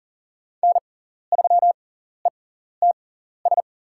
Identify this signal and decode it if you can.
Morse code, heard as a keyed tone, about 38 wpm, spaced as for 14 wpm, 700 Hz. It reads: N3ETS
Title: USA Call Signs - 6x Character Spacing - 40wpm